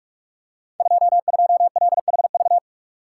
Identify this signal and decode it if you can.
Morse code, heard as a keyed tone, about 46 wpm, 700 Hz. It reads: 22LHV
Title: Code Groups - 5 Letter:Numbers - 45wpm